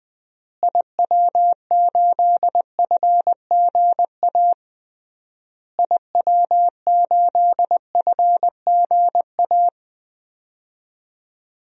Morse code, heard as a keyed tone, about 20 wpm, 700 Hz. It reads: IW8FGA IW8FGA
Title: DXCC-100 Call Signs - 6 Characters - Repeated 3x - 20wpm